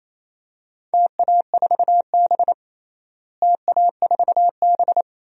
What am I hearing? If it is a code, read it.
Morse code, heard as a keyed tone, about 27 wpm, 700 Hz. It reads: TA4B TA4B
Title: DXCC-100 Call Signs - 4 Characters - Repeated 2x - 28wpm